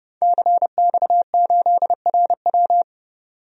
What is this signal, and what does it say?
Morse code, heard as a keyed tone, about 32 wpm, 700 Hz. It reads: CX8RW
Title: DXCC-100 Call Signs - 5 Characters - 30wpm